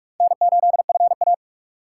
Morse code, heard as a keyed tone, about 60 wpm, spaced as for 52 wpm, 700 Hz. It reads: N8FA
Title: Speed Racing - USA 1x2 Call Signs 45wpm